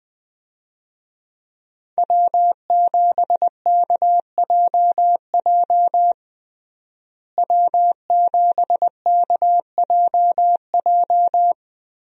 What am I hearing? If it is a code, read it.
Morse code, heard as a keyed tone, about 21 wpm, 700 Hz. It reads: W7KJJ W7KJJ
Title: USA Call Signs - 5 Characters - Repeated 2x - 20wpm